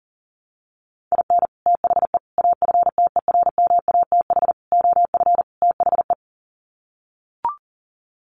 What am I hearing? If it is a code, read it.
Morse code, heard as a keyed tone, about 40 wpm, 700 Hz. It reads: IN THE AFTERMATH OF THE
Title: Sets of 5 Words 40wpm